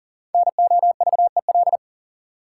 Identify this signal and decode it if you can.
Morse code, heard as a keyed tone, about 40 wpm, 700 Hz. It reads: NOVEL
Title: Sentences from Top 1300 Words - New Word Drill - 40wpm